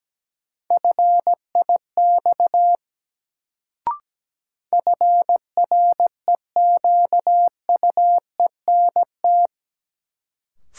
Morse code, heard as a keyed tone, about 17 wpm, 700 Hz. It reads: FIX FREQUENT
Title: Sentences from Top 1400 Words - New Word Drill - 17wpm